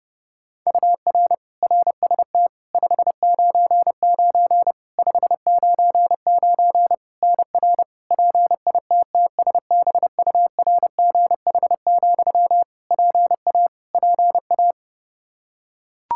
Morse code, heard as a keyed tone, about 31 wpm, 700 Hz. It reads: UR RST 599 599 NR PITTSBURGH, PA PA
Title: Practice QSO - Exchange 1 - Beginning - 30wpm